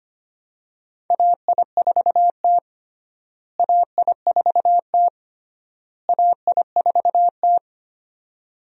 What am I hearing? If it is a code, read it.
Morse code, heard as a keyed tone, about 25 wpm, 700 Hz. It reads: AI4T AI4T AI4T
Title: USA Call Signs - 4 Characters - Repeated 3x - 25wpm